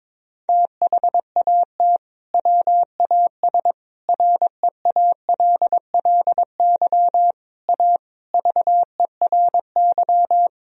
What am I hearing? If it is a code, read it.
Morse code, heard as a keyed tone, about 21 wpm, 700 Hz. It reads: THAT WAS REALLY A VERY
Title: Sentences from Top 1000 Words - Review Part 1 - 22wpm